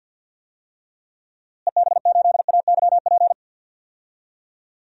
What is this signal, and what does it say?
Morse code, heard as a keyed tone, about 49 wpm, 700 Hz. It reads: EB8AYP
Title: DXCC-100 Call Signs - 6 Characters - Repeated 3x - 50wpm